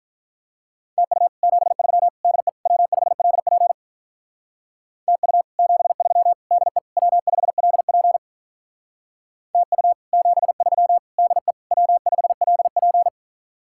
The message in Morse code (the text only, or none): TU 73 DE W5LP TU 73 DE W5LP TU 73 DE W5LP